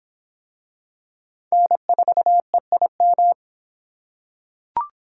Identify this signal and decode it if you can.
Morse code, heard as a keyed tone, about 23 wpm, 700 Hz. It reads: N4EIM